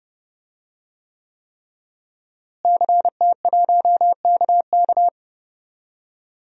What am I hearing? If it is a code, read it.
Morse code, heard as a keyed tone, about 32 wpm, 700 Hz. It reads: CT1KK
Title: DXCC-100 Call Signs - 5 Characters - Repeated 3x - 30wpm